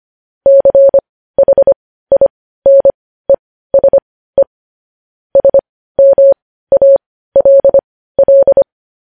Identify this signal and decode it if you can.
Morse code, heard as a keyed tone, about 25 wpm, spaced as for 15 wpm, 550 Hz. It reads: CHINESE SMALL
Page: news headlines in morse code at 15 wpm